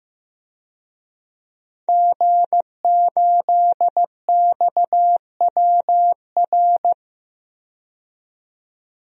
Morse code, heard as a keyed tone, about 15 wpm, 700 Hz. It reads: G8XWR
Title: DXCC-100 Call Signs - 5 Characters - 15wpm